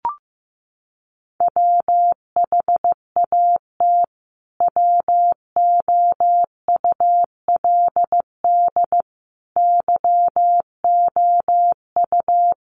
Morse code, heard as a keyed tone, about 15 wpm, 700 Hz. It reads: WHAT WOULD YOU
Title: Sentences from Top 200 Words 15wpm